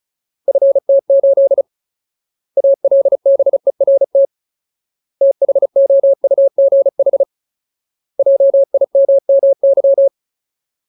Morse code, heard as a keyed tone, about 34 wpm, 550 Hz. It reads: FT8 ALBERT THOUGH JIMMY
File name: Ultimate Word List / 35WPM Ultimate